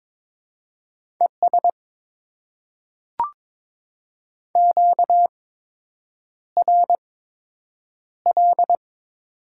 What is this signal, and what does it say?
Morse code, heard as a keyed tone, about 22 wpm, spaced as for 8 wpm, 700 Hz. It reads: ES QRL
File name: Top 20 QSO Elements - 8x Character Spacing - 22wpm